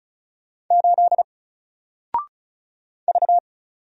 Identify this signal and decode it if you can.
Morse code, heard as a keyed tone, about 36 wpm, 700 Hz. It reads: 8 V
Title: Single Character (Letter:Number:Punctuation:Prosign) Alt 35wpm